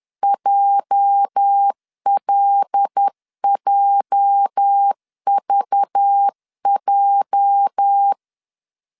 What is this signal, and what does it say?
Morse code, heard as a keyed tone, about 10 wpm, 800 Hz. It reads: JLJVJ
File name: LEZIONI / 8 YJ45 / 8 YJ 10wpm 2651